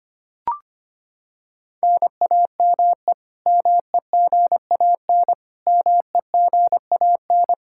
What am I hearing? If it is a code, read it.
Morse code, heard as a keyed tone, about 25 wpm, 700 Hz. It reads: NAME MEGAN MEGAN
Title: Practice QSO - Exchange 1 - Beginning - 25wpm